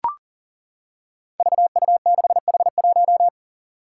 Morse code, heard as a keyed tone, about 40 wpm, 700 Hz. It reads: VU6H1
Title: Five Challenging Letter-Number Combinations 40wpm